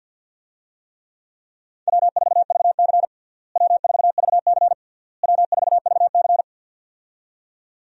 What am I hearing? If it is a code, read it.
Morse code, heard as a keyed tone, about 52 wpm, 700 Hz. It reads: W4VC W4VC W4VC
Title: USA Call Signs - 4 Characters - Repeated 3x - 50wpm